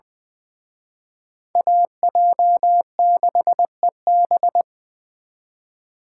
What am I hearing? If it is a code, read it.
Morse code, heard as a keyed tone, about 24 wpm, 700 Hz. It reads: AJ6EB
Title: Speed Racing - USA 2x2 Call Signs 20wpm